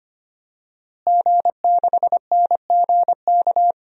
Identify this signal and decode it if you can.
Morse code, heard as a keyed tone, about 25 wpm, 700 Hz. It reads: G6NGK
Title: DXCC-100 Call Signs - 5 Characters - Repeated 3x - 25wpm